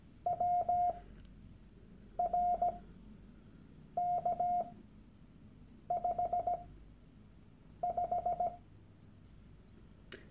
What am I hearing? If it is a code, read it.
Morse code, heard as a keyed tone, about 17 wpm, spaced as for 6 wpm, 710 Hz. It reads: WRK55
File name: Workshop Dateien / CW-Online-Kurs / Lektion 001